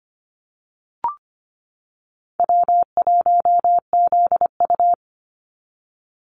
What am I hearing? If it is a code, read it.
Morse code, heard as a keyed tone, about 25 wpm, 690 Hz. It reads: W1ZU
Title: NA SOTA Activator Callsigns 25wpm